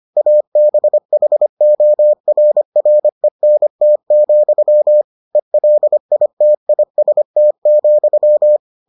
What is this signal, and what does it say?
Morse code, heard as a keyed tone, about 25 wpm, 600 Hz. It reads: ABHORRENT, ELITIST,